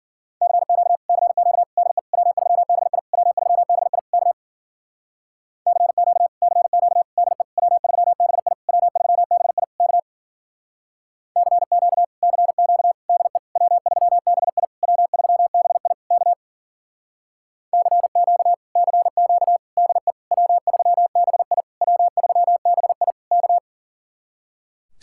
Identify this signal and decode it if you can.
Morse code, heard as a keyed tone, about 51 wpm, 700 Hz. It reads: CQ CQ DE W3BI W3BI K CQ CQ DE W3BI W3BI K CQ CQ DE W3BI W3BI K CQ CQ DE W3BI W3BI K